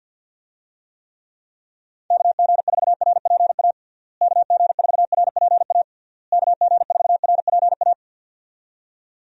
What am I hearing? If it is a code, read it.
Morse code, heard as a keyed tone, about 52 wpm, 700 Hz. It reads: KG4RPA KG4RPA KG4RPA
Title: USA Call Signs - 6 Characters - Repeated 3x - 50wpm